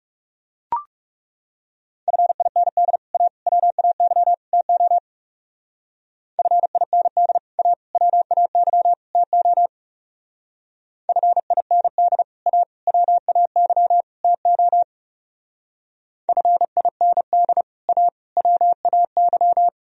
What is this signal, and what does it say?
Morse code, heard as a keyed tone, about 39 wpm, 700 Hz. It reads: FIND A WAY TO FIND A WAY TO FIND A WAY TO FIND A WAY